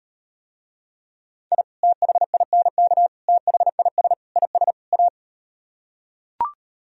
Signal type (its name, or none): Morse code